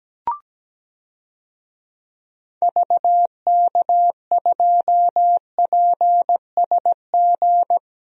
Morse code, heard as a keyed tone, about 17 wpm, 700 Hz. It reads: VK2PSG
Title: DXCC-100 Call Signs - 6 Characters - Repeated 2x - 17wpm